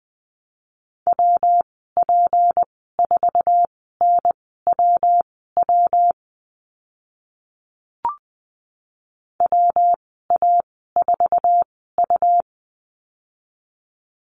Morse code, heard as a keyed tone, about 20 wpm, spaced as for 14 wpm, 700 Hz. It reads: WP4NWW WA4U
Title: 154 Lesson 25g - Letter p - Call Signs 20wpm